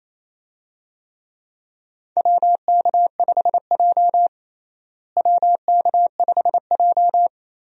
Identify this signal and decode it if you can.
Morse code, heard as a keyed tone, about 28 wpm, 700 Hz. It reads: WK5J WK5J
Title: USA Call Signs - 4 Characters - Repeated 3x - 28wpm